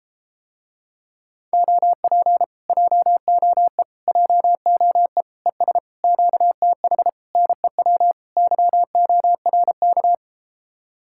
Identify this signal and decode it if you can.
Morse code, heard as a keyed tone, about 33 wpm, 700 Hz. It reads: OP JOE JOE ES QTH NEW YORK